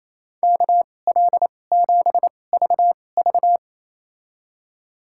Morse code, heard as a keyed tone, about 28 wpm, spaced as for 20 wpm, 700 Hz. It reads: KL7VV
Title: USA Call Signs - 5 Characters - 2x Character Spacing - 28wpm